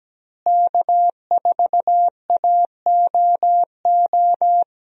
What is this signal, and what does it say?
Morse code, heard as a keyed tone, about 17 wpm, 700 Hz. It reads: K4AOO
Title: USA Call Signs - 5 Characters - Repeated 3x - 17wpm